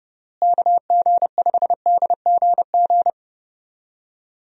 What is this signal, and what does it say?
Morse code, heard as a keyed tone, about 32 wpm, 700 Hz. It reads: KG5DGG
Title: USA Call Signs - 6 Characters - Repeated 3x - 30wpm